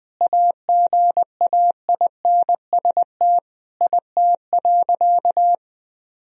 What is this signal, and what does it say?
Morse code, heard as a keyed tone, about 20 wpm, 700 Hz. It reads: AGAINST IT.
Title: Sentences from Top 1000 Words - Review Part 1 - 20wpm